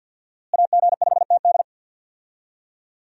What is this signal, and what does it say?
Morse code, heard as a keyed tone, about 47 wpm, 700 Hz. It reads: AG5TD
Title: USA Call Signs - 5 Characters - Repeated 2x - 50wpm